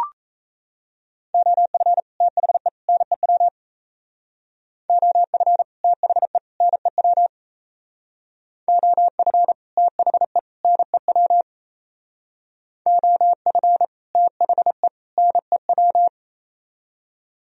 Morse code, heard as a keyed tone, about 35 wpm, 700 Hz. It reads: OF THE NEW OF THE NEW OF THE NEW OF THE NEW
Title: Speed-Racing - Sets of 3 Words 28wpm